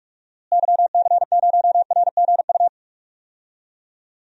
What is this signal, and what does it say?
Morse code, heard as a keyed tone, about 42 wpm, 700 Hz. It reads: YC0RGU
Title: DXCC-100 Call Signs - 6 Characters - Repeated 2x - 45wpm